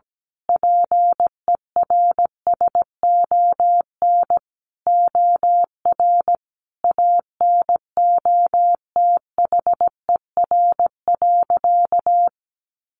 Morse code, heard as a keyed tone, about 17 wpm, 700 Hz. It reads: PERSON OR ANOTHER.
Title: Sentences from Top 1400 Words 17wpm